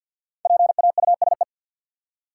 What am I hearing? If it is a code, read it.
Morse code, heard as a keyed tone, about 51 wpm, 700 Hz. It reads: PAUSE
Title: Five Letter Words - 50wpm